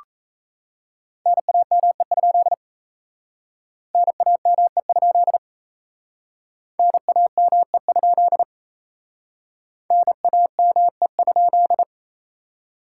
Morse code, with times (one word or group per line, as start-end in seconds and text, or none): 1.3-2.5 NAME?
3.9-5.4 NAME?
6.8-8.4 NAME?
9.9-11.8 NAME?